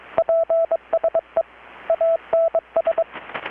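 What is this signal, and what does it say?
Morse code, heard as a keyed tone, about 22 wpm, 660 Hz. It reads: PSE ANS
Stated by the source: Morse from KPH's Night of Nights XX maritime radiotelegraphy event, received in Chiapas, Mexico about 3,800 km away on a Kenwood R-600 with a grounded 10 m long-wire antenna.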